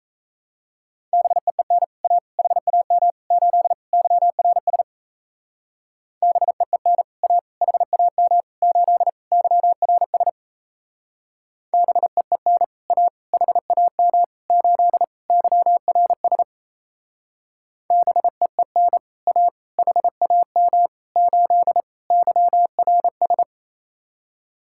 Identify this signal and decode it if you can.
Morse code, heard as a keyed tone, about 36 wpm, 700 Hz. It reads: BEEN A HAM 8 YRS BEEN A HAM 8 YRS BEEN A HAM 8 YRS BEEN A HAM 8 YRS